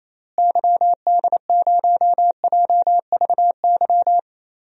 Morse code, heard as a keyed tone, about 28 wpm, 700 Hz. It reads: YD0JVY
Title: DXCC-100 Call Signs - 6 Characters - Repeated 2x - 28wpm